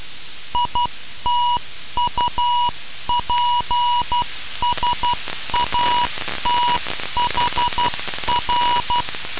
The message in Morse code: ITUPSAT HR